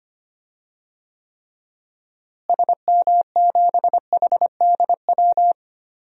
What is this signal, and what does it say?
Morse code, heard as a keyed tone, about 25 wpm, 700 Hz. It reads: SM7HDW
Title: DXCC-100 Call Signs - 6 Characters - Repeated 2x - 25wpm